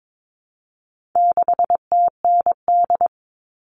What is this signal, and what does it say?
Morse code, heard as a keyed tone, about 22 wpm, 700 Hz. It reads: 6TND
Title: All Cut Numbers - 4-Digit 22wpm